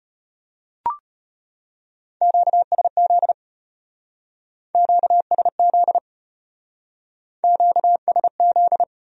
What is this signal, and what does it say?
Morse code, heard as a keyed tone, about 33 wpm, 700 Hz. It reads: QSZ QSZ QSZ